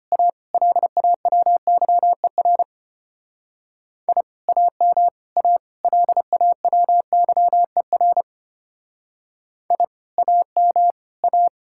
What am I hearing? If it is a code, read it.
Morse code, heard as a keyed tone, about 31 wpm, 700 Hz. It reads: A LAWYER I AM A LAWYER I AM A
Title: Speed-Racing Common QSO Phrases Encore - 25wpm